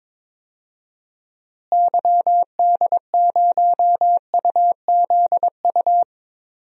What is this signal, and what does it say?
Morse code, heard as a keyed tone, about 22 wpm, 700 Hz. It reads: YD0UZU
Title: DXCC-100 Call Signs - 6 Characters - Repeated 3x - 22wpm